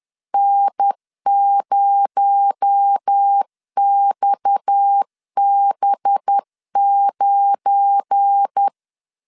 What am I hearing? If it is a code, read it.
Morse code, heard as a keyed tone, about 10 wpm, 800 Hz. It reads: N0XB9
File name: CW / DET / 1 9 1500C S C 10wpm 4557